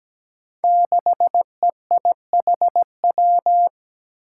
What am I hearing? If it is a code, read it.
Morse code, heard as a keyed tone, about 17 wpm, 700 Hz. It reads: 6EIHW
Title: Five Challenging Letter-Number Combinations 17wpm